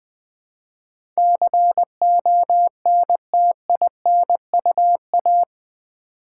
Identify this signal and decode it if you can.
Morse code, heard as a keyed tone, about 20 wpm, 700 Hz. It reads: CONTINUA
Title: Build A Word - 4 Letters At A Time - 11-Letter Words - 20wpm